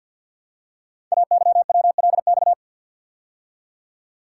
Morse code, heard as a keyed tone, about 50 wpm, 700 Hz. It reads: AYWLX
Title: Code Groups - 5 Letters - 50wpm